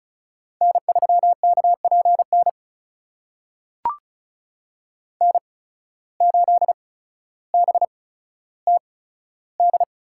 Morse code, heard as a keyed tone, about 36 wpm, spaced as for 15 wpm, 700 Hz. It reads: N3KPN N8BTD
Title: USA Call Signs - 5 Characters - 8x Character Spacing - 35wpm